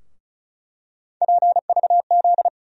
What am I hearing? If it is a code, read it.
Morse code, heard as a keyed tone, about 36 wpm, 700 Hz. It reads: PVZ
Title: USA Call Sign Postfixes - 3 Letters 35wpm